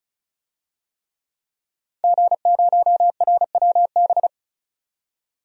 Morse code, heard as a keyed tone, about 34 wpm, 700 Hz. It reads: G0RWB
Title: DXCC-100 Call Signs - 5 Characters - 35wpm